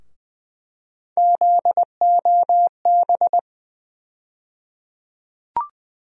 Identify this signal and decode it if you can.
Morse code, heard as a keyed tone, about 21 wpm, 700 Hz. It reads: ZOB
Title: USA Call Sign Postfixes - 3 Letters 20wpm